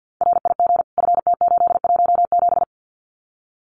Morse code, heard as a keyed tone, about 50 wpm, 700 Hz. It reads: RIG FT817
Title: Common QSO Phrases Encore 50wpm